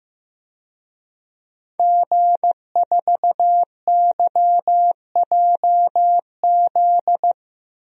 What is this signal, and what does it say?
Morse code, heard as a keyed tone, about 15 wpm, 700 Hz. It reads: G4YJZ
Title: DXCC-100 Call Signs - 5 Characters - Repeated 3x - 15wpm